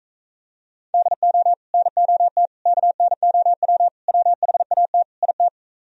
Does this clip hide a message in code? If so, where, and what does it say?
Yes: Morse code from 0.9 to 5.5 s, heard as a keyed tone, about 45 wpm, 700 Hz. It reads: DO NOT KNOW WHAT IT